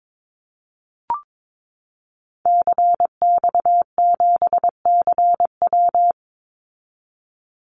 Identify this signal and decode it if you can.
Morse code, heard as a keyed tone, about 22 wpm, 700 Hz. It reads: CX7CW
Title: Call Signs Repeated 3x - 22wpm